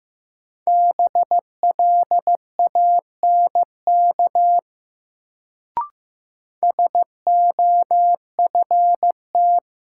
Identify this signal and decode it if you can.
Morse code, heard as a keyed tone, about 15 wpm, 700 Hz. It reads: BLANK SOFT
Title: Sentences from Top 1500 Words - New Word Drill - 15wpm